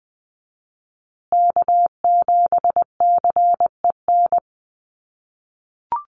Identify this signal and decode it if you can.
Morse code, heard as a keyed tone, about 20 wpm, 700 Hz. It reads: K7CEN
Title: USA Call Signs Repeated 3x 20wpm